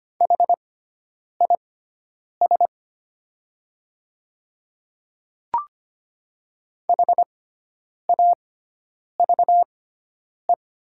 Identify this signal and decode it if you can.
Morse code, heard as a keyed tone, about 25 wpm, spaced as for 9 wpm, 690 Hz. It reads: HIS HAVE
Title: 6x Character Spacing - Top 100 Words - 25wpm